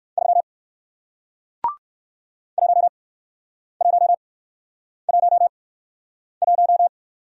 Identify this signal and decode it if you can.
Morse code, heard as a keyed tone, about 57 wpm, spaced as for 9 wpm, 700 Hz. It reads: 3 1111